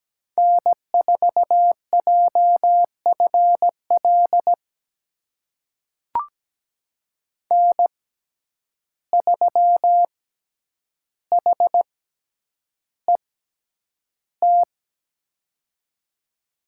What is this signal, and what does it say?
Morse code, heard as a keyed tone, about 17 wpm, spaced as for 9 wpm, 700 Hz. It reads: N4JFL N3HET
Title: USA Call Signs - 5 Characters - 6x Character Spacing - 17wpm